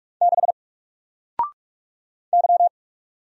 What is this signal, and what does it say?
Morse code, heard as a keyed tone, about 44 wpm, 700 Hz. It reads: 6 Y
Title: Single Letter-Number 45wpm